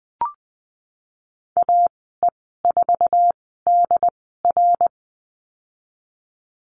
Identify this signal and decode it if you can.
Morse code, heard as a keyed tone, about 20 wpm, spaced as for 15 wpm, 700 Hz. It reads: AE4DR
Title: 088 Lesson 15g - Number 5 - Call Signs 20wpm